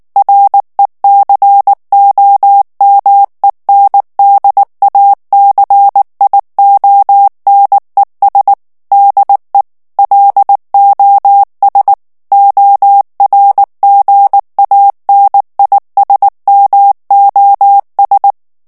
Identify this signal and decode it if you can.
Morse code, heard as a keyed tone, about 19 wpm, 800 Hz. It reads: RECOMENDACIONES DE LOS ORGANISMOS
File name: morse mp3 / ESPAGNOL TEXTE EN CLAIR 2 VITESSE 18 MOTS MINUTES